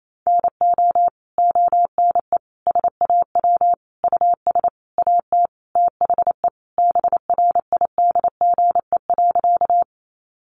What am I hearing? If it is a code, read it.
Morse code, heard as a keyed tone, about 28 wpm, 700 Hz. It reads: NO ONE SAW US AT THE BRIDGE.
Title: Sentences from Top 1200 Words 28wpm